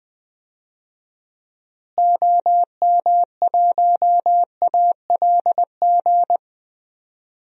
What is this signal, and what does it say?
Morse code, heard as a keyed tone, about 20 wpm, 700 Hz. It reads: OM1ALG
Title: DXCC-100 Call Signs - 6 Characters - Repeated 2x - 20wpm